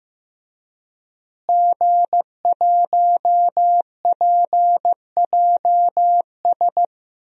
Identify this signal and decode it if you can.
Morse code, heard as a keyed tone, about 15 wpm, 700 Hz. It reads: G1PJS
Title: DXCC-100 Call Signs - 5 Characters - Repeated 3x - 15wpm